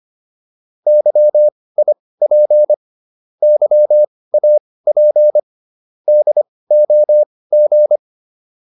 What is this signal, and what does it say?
Morse code, heard as a keyed tone, about 25 wpm, spaced as for 18 wpm, 600 Hz. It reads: YIP YAP DOG